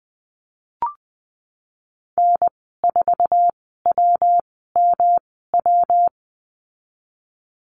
Morse code, heard as a keyed tone, about 19 wpm, spaced as for 14 wpm, 700 Hz. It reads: N4WMW
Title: 147 Lesson 24g - Letter y - Call Signs 20wpm